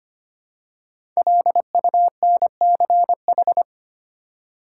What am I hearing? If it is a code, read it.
Morse code, heard as a keyed tone, about 25 wpm, 700 Hz. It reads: LUNCH
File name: 5 Letter Words - First 3 Letters Given - 25wpm